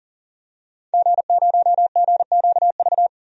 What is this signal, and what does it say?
Morse code, heard as a keyed tone, about 41 wpm, 700 Hz. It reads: G0GQV
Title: DXCC-100 Call Signs - 5 Characters - 40wpm